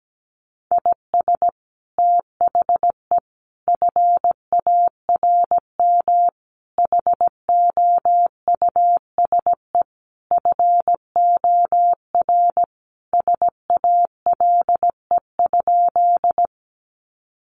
Morse code, heard as a keyed tone, about 17 wpm, 700 Hz. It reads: IS THE FARM HOUSE FOR SALE?
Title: Sentences from Top 1400 Words 17wpm